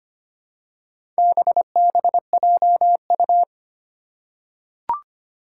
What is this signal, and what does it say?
Morse code, heard as a keyed tone, about 25 wpm, 700 Hz. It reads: BBJU
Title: Four Challenging Letter-Number Combinations 25wpm